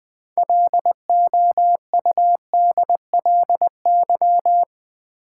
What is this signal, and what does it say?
Morse code, heard as a keyed tone, about 20 wpm, 700 Hz. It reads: LOUDLY
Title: 6 Letter Words - First 3 Letters Given - 20wpm